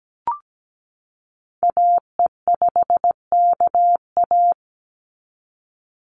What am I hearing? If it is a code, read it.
Morse code, heard as a keyed tone, about 17 wpm, 700 Hz. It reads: AE5KA
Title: NA SOTA Activator Callsigns 17wpm